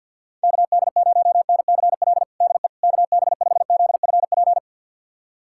Morse code, heard as a keyed tone, about 51 wpm, 700 Hz. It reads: KD0NCL DE KB5ZFP
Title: USA Callsign DE Callsign - 6 Characters - 50wpm